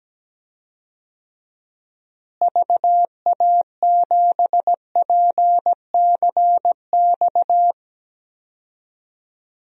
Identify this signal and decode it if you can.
Morse code, heard as a keyed tone, about 17 wpm, 700 Hz. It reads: VA7PCX
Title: DXCC-100 Call Signs - 6 Characters - Repeated 3x - 17wpm